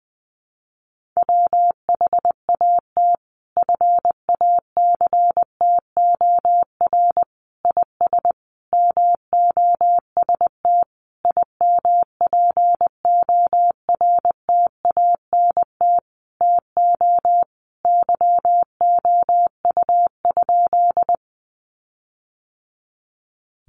Morse code, heard as a keyed tone, about 20 wpm, 700 Hz. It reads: WHAT FACTOR IS MOST IMPORTANT TO YOU?
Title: Sentences from Top 1300 Words 20wpm